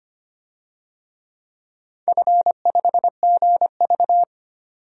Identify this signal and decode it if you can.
Morse code, heard as a keyed tone, about 25 wpm, 700 Hz. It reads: F5GV
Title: DXCC-100 Call Signs - 4 Characters - 25wpm